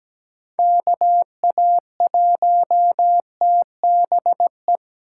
Morse code, heard as a keyed tone, about 17 wpm, 700 Hz. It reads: KA1TBE
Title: USA Call Signs - 6 Characters - 2x Character Spacing - 17wpm